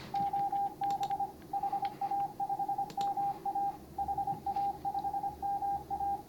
Morse code, heard as a keyed tone, about 25 wpm, 800 Hz, with radio noise. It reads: OB HA5MA HA5MA